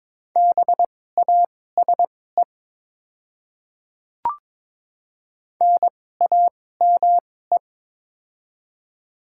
Morse code, heard as a keyed tone, about 22 wpm, spaced as for 16 wpm, 700 Hz. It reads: BASE NAME
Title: 2x Character Spacing - Top 300 Words - 22wpm